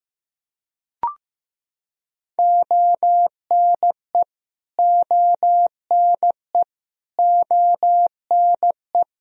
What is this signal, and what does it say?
Morse code, heard as a keyed tone, about 15 wpm, 700 Hz. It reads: ONE ONE ONE
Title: Top 100 Words Repeated 15wpm